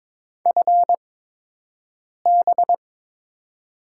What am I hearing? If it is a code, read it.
Morse code, heard as a keyed tone, about 23 wpm, spaced as for 6 wpm, 700 Hz. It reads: FB
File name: Top 90 QSO Elements - 8x Character Spacing - 22wpm